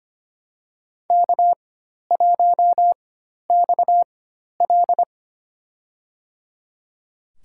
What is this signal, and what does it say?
Morse code, heard as a keyed tone, about 25 wpm, spaced as for 12 wpm, 700 Hz. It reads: K1XL
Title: USA Call Signs - 4 Characters - 4x Character Spacing - 25wpm